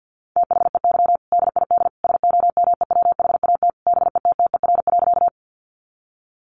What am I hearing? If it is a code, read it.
Morse code, heard as a keyed tone, about 50 wpm, 700 Hz. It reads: THEY DID SOMEWHAT BETTER.
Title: Sentences from Top 1300 Words 50wpm